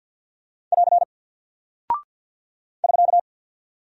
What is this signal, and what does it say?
Morse code, heard as a keyed tone, about 48 wpm, 700 Hz. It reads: <AR> <SK>
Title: Single Character (Letter:Number:Punctuation:Prosign) Alt 50wpm